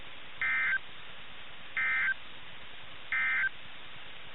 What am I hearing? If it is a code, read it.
Emergency Alert System (SAME) tones, heard as harsh buzzing bursts. It is heard from 0.3 to 3.6 s.
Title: NOAA Weather Radio - KIH61